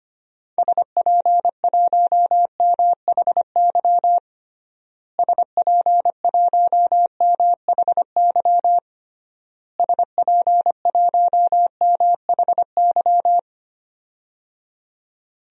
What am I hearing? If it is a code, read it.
Morse code, heard as a keyed tone, about 25 wpm, 700 Hz. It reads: SP1MHY SP1MHY SP1MHY
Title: DXCC-100 Call Signs - 6 Characters - Repeated 3x - 25wpm